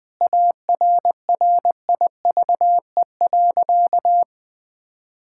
Morse code, heard as a keyed tone, about 20 wpm, 700 Hz. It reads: ARRIVE.
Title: Sentences with 4-Word Phrases - 20wpm